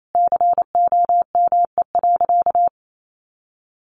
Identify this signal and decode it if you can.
Morse code, heard as a keyed tone, about 28 wpm, 700 Hz. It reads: COME.
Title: Sentences from Top 1500 Words 28wpm